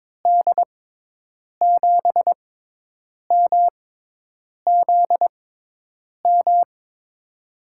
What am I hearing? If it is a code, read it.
Morse code, heard as a keyed tone, about 22 wpm, spaced as for 8 wpm, 700 Hz. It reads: D7MZM
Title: USA Call Signs - 6 Characters - 6x Character Spacing - 22wpm